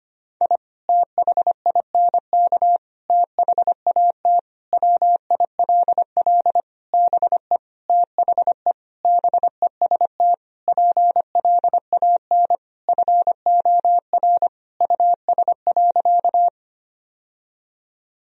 Morse code, heard as a keyed tone, about 25 wpm, 700 Hz. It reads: I THINK THAT WILL BE THE BEST PLAN FOR US.
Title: Sentences from Top 1000 Words - Review Part 2 - 25wpm